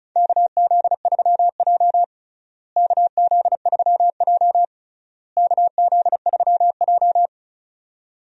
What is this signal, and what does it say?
Morse code, heard as a keyed tone, about 36 wpm, 700 Hz. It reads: KZ3J KZ3J KZ3J
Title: USA Call Signs - 4 Characters - Repeated 3x - 35wpm